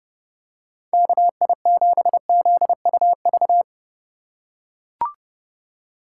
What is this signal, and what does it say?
Morse code, heard as a keyed tone, about 29 wpm, 700 Hz. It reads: KI7ZUV